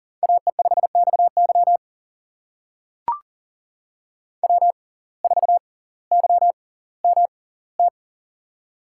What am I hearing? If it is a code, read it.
Morse code, heard as a keyed tone, about 40 wpm, spaced as for 20 wpm, 700 Hz. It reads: AE5XY W4YMT
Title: USA Call Signs - 5 Characters - 6x Character Spacing - 40wpm